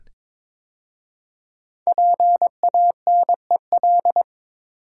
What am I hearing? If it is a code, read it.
Morse code, heard as a keyed tone, about 22 wpm, 700 Hz. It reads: PANEL
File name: 5 Letter Words - First 3 Letters Given - 22wpm